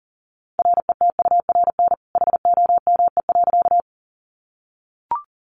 Morse code, heard as a keyed tone, about 40 wpm, 700 Hz. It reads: RETURN HOME.
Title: Sentences from Top 800 Words 40wpm